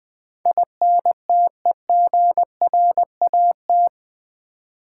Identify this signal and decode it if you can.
Morse code, heard as a keyed tone, about 20 wpm, 700 Hz. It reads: INTEGRAT
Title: Build A Word - 4 Letters At A Time - 10-Letter Words - 20wpm